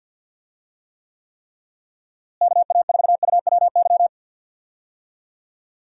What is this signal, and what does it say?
Morse code, heard as a keyed tone, about 52 wpm, 700 Hz. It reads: KA4UWY
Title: USA Call Signs - 6 Characters - Repeated 3x - 50wpm